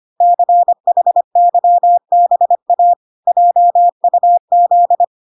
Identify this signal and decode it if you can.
Morse code, heard as a keyed tone, about 25 wpm, 700 Hz. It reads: CHYBA JUZ